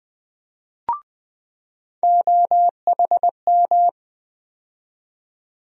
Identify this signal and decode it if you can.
Morse code, heard as a keyed tone, about 21 wpm, 700 Hz. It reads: OHM
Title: USA Call Sign Postfixes - 3 Letters 20wpm